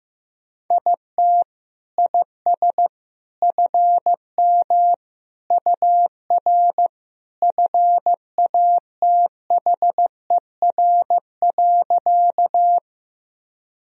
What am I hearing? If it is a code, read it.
Morse code, heard as a keyed tone, about 15 wpm, 700 Hz. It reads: IT IS FM UR FATHER.
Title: Sentences from Top 300 Words and Abbreviations 15wpm